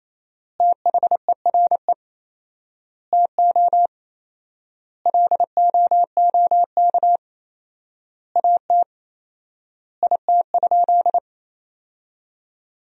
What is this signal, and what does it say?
Morse code, heard as a keyed tone, about 28 wpm, 700 Hz. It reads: THERE TO LOOK AT IT?
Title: 4x Word-Spacing - Sentences from Top 100 Words - 28wpm